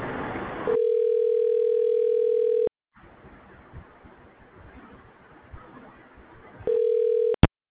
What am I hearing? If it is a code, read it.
Ringback tone (North America), 440 + 480 Hz.